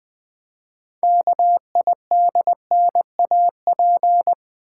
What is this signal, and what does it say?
Morse code, heard as a keyed tone, about 20 wpm, 700 Hz. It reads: KIDNAP
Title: 6 Letter Words - First 3 Letters Given - 20wpm